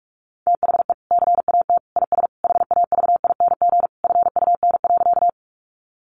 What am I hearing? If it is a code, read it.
Morse code, heard as a keyed tone, about 44 wpm, 700 Hz. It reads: THE CAT IS HAVING FUN.
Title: Easy Phrases 45wpm